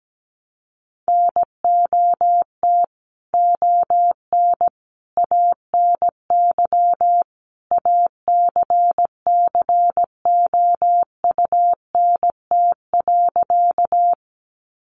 Morse code, heard as a keyed tone, about 17 wpm, 700 Hz. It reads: NOT ON ANY ACCOUNT.
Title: Sentences from Top 900 Words 17wpm